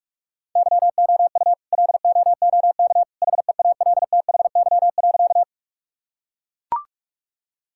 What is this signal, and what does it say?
Morse code, heard as a keyed tone, about 44 wpm, 700 Hz. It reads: YOU LOOK HEALTHY.
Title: Sentences from Top 1000 Words - Review Part 5 - 45wpm